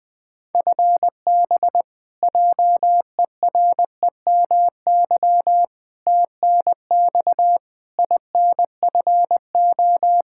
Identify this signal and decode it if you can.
Morse code, heard as a keyed tone, about 20 wpm, 700 Hz. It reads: FB JEREMY TNX INFO
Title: Practice QSO - Exchange 3 - More Info - 20wpm